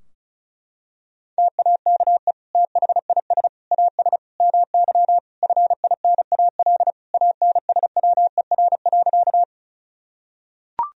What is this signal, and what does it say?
Morse code, heard as a keyed tone, about 36 wpm, 700 Hz. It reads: TAKE THIS AS MY FINAL ANSWER.
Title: Sentences from Top 1000 Words - Review Part 2 - 35wpm